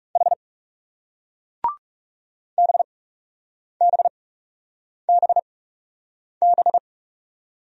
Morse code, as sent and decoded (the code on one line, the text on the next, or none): .... / -... -... -... -...
H BBBB